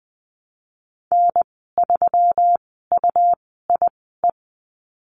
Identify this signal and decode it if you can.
Morse code, heard as a keyed tone, about 20 wpm, spaced as for 15 wpm, 700 Hz. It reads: N3UIE
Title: 161 Lesson 26g - Letter g - Call Signs 20wpm